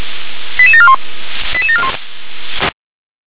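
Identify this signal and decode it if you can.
Five-tone selective call (ZVEI), heard as quick runs of five tones, tones of about 70 ms. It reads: 88741 88741
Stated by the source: ZVEI five-tone call received by SDR.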